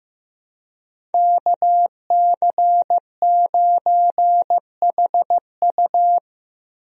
Morse code, heard as a keyed tone, about 15 wpm, 700 Hz. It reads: KC9HU
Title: USA Call Signs - 5 Characters - Repeated 3x - 15wpm